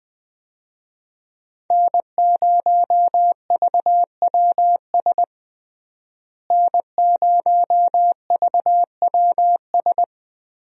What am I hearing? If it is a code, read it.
Morse code, heard as a keyed tone, about 20 wpm, 700 Hz. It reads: N0VWS N0VWS
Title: USA Call Signs - 5 Characters - Repeated 3x - 20wpm